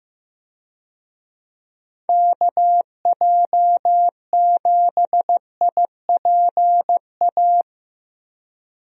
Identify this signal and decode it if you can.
Morse code, heard as a keyed tone, about 15 wpm, 700 Hz. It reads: KJ7IPA
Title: USA Call Signs - 6 Characters - Repeated 2x - 15wpm